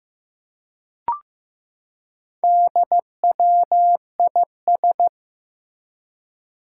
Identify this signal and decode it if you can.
Morse code, heard as a keyed tone, about 15 wpm, 700 Hz. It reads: DWIS
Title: Four Challenging Letter-Number Combinations 15wpm